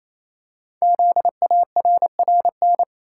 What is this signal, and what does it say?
Morse code, heard as a keyed tone, about 28 wpm, 700 Hz. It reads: ZARRN
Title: Code Groups - 5 Letters - 28wpm